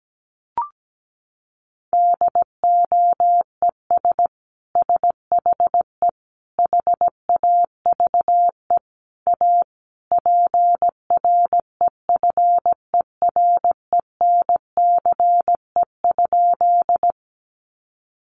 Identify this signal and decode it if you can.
Morse code, heard as a keyed tone, about 17 wpm, 700 Hz. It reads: DOES SHE HAVE A PREFERENCE?
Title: Sentences from Top 1500 Words 17wpm